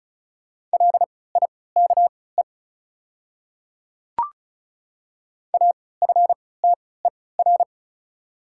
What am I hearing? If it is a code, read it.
Morse code, heard as a keyed tone, about 34 wpm, spaced as for 20 wpm, 700 Hz. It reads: LIKE AFTER
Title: 3x Character Spacing - Top 300 Words - 35wpm